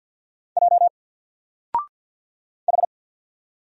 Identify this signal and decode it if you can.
Morse code, heard as a keyed tone, about 48 wpm, 700 Hz. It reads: J H
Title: Single Character (Letter:Number:Punctuation:Prosign) 50wpm